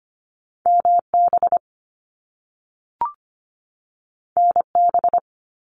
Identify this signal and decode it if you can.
Morse code, heard as a keyed tone, about 25 wpm, 700 Hz. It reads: MB NB
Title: Canadian Provinces and Territories 25wpm